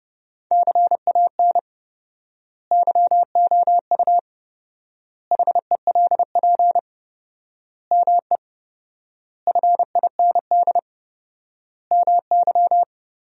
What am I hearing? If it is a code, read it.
Morse code, heard as a keyed tone, about 30 wpm, 700 Hz. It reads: CAN YOU HELP ME FIND MY
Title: 4x Word-Spacing - Sentences from Top 200 Words - 30wpm